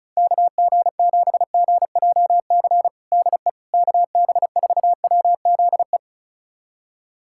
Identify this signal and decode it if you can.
Morse code, heard as a keyed tone, about 34 wpm, 700 Hz. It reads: KG7GJC DE KB4WZE
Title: USA Callsign DE Callsign 35wpm